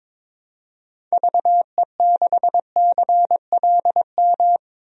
Morse code, heard as a keyed tone, about 21 wpm, 700 Hz. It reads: VE6CLM
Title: DXCC-100 Call Signs - 6 Characters - 22wpm